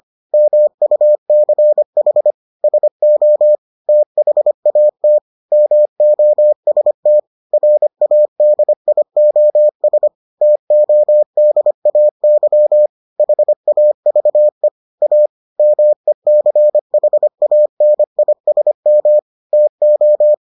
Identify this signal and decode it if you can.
Morse code, heard as a keyed tone, about 25 wpm, 600 Hz. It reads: MUCH SO THAT MOST RADIOS TODAY HAVE A MECHANISM TO